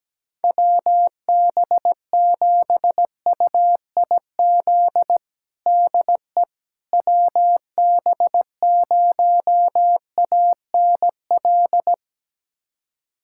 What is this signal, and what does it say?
Morse code, heard as a keyed tone, about 17 wpm, 700 Hz. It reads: WB7UIZ DE WB0ANL
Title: USA Callsign DE Callsign - 6 Characters - 17wpm